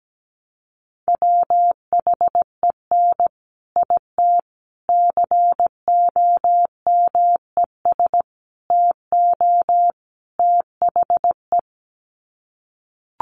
Morse code, heard as a keyed tone, about 17 wpm, 700 Hz. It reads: WHEN IT COMES TO THE
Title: Sets of 5 Words 17wpm